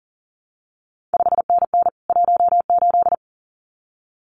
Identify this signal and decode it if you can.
Morse code, heard as a keyed tone, about 39 wpm, 700 Hz. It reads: 5NN 18
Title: CQ WW DX Contest Exchange Encore - 40wpm